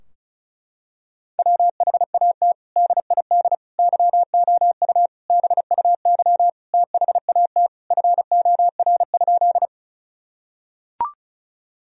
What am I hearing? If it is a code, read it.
Morse code, heard as a keyed tone, about 34 wpm, 700 Hz. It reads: WHAT DID YOU BUY THAT FOR?
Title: Sentences from Top 1000 Words - Review Part 4 - 35wpm